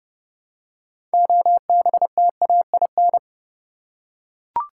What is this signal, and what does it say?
Morse code, heard as a keyed tone, about 30 wpm, 700 Hz. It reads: OBTAIN
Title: Six Letter Words - 30wpm